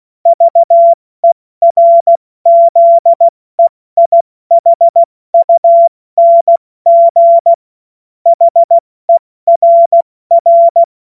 Morse code, heard as a keyed tone, about 16 wpm, 680 Hz. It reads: VERZEIHUNG HERR
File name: Witz-30-September-2025-70-F80